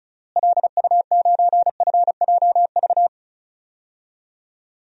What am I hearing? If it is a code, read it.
Morse code, heard as a keyed tone, about 36 wpm, 700 Hz. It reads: LU9FJV
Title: DXCC-100 Call Signs - 6 Characters - Repeated 3x - 35wpm